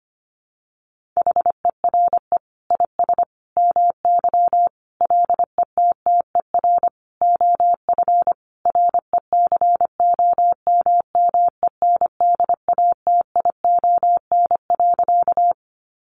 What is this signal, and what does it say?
Morse code, heard as a keyed tone, about 25 wpm, 700 Hz. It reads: HERE IS MY LETTER OF RECOMMENDATION.
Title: Sentences from Top 1400 Words 25wpm